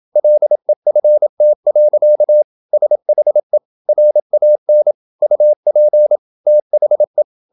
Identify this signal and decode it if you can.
Morse code, heard as a keyed tone, about 27 wpm, 600 Hz. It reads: LEFT. SHE RAN UP THE
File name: Never 0027wpm